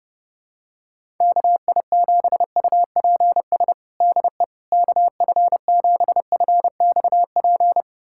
Morse code, heard as a keyed tone, about 30 wpm, 700 Hz. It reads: KI7UPS DE KF7FXP
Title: USA Callsign DE Callsign - 6 Characters - 30wpm